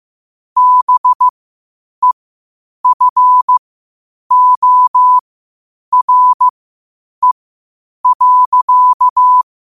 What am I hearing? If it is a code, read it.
Morse code, heard as a keyed tone, about 15 wpm, spaced as for 8 wpm, 1000 Hz. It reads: BEFORE.
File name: qotdfiles / 8wpm